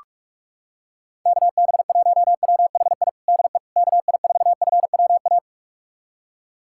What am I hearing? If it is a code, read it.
Morse code, heard as a keyed tone, about 46 wpm, 700 Hz. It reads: KB1WHI DE KI4FWA
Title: USA Callsign DE Callsign - 6 Characters - 45wpm